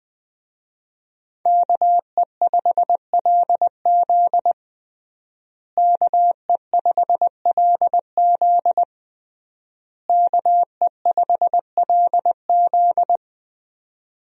Morse code, heard as a keyed tone, about 20 wpm, 700 Hz. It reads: KE5LZ KE5LZ KE5LZ
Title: USA Call Signs - 5 Characters - Repeated 3x - 20wpm